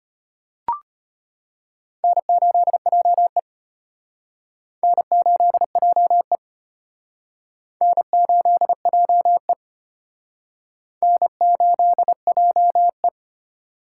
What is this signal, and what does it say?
Morse code, heard as a keyed tone, about 31 wpm, 700 Hz. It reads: N8JE N8JE N8JE N8JE
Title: Speed Racing - USA 1x2 Call Signs 25wpm